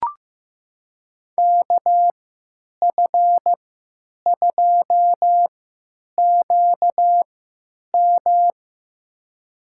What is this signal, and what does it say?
Morse code, heard as a keyed tone, about 15 wpm, spaced as for 9 wpm, 700 Hz. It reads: KF2QM
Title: USA Call Signs - 5 Characters - 3x Character Spacing - 15wpm